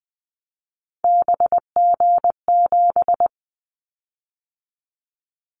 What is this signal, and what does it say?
Morse code, heard as a keyed tone, about 20 wpm, 700 Hz. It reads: BG7
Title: Callsign Prefixes for Top 100 DXCC - 20wpm